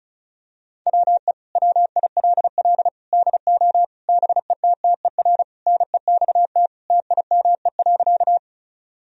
Morse code, heard as a keyed tone, about 34 wpm, 700 Hz. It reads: WE WILL DO BETTER NEXT TIME.
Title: Sentences from Top 1000 Words - Review Part 1 - 35wpm